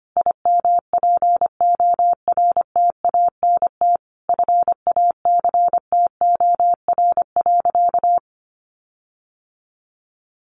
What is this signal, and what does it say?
Morse code, heard as a keyed tone, about 25 wpm, 700 Hz. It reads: IMPORTANT FACTOR.
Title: Sentences from Top 1300 Words 25wpm